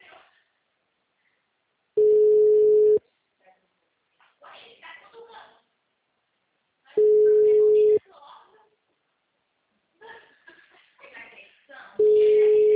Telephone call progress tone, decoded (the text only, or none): ringback tone